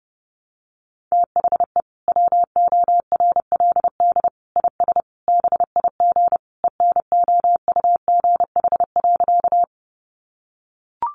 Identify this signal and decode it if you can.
Morse code, heard as a keyed tone, about 30 wpm, 700 Hz. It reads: THE WORLD IS BIG ENOUGH.
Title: Sentences from Top 400 Words 30wpm